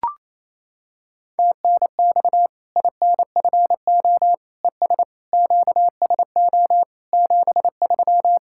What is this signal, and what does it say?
Morse code, heard as a keyed tone, about 28 wpm, 700 Hz. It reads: TNX INFO ES QSO 73
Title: K1USN SST QSO Practice 28wpm